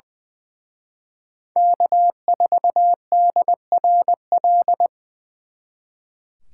Morse code, heard as a keyed tone, about 24 wpm, 700 Hz. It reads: K4DRL
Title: Speed Racing - USA 1x3 Call Signs 20wpm